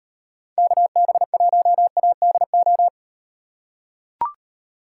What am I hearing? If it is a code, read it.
Morse code, heard as a keyed tone, about 36 wpm, 700 Hz. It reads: KB1ADO